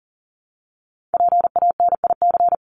Morse code, heard as a keyed tone, about 40 wpm, 700 Hz. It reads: PANIC
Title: Top 1751-2000 Words 40wpm